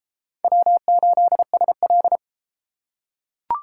Morse code, heard as a keyed tone, about 30 wpm, 700 Hz. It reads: W8SL